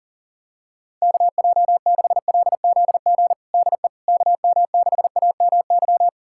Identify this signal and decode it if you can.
Morse code, heard as a keyed tone, about 40 wpm, 700 Hz. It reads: KJ6LZG DE KM6AMY
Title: USA Callsign DE Callsign - 6 Characters - 40wpm